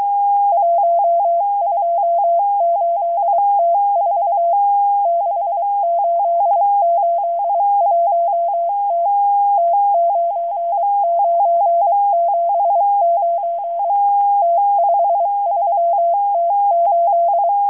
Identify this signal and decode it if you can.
Morse code, heard as a keyed tone, about 23 wpm, 680 Hz. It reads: 128T4 6881T T9979 T53T8